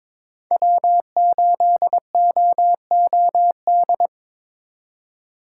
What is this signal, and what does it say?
Morse code, heard as a keyed tone, about 23 wpm, 700 Hz. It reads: W8OOD
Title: USA Call Signs - 5 Characters - 3x Character Spacing - 22wpm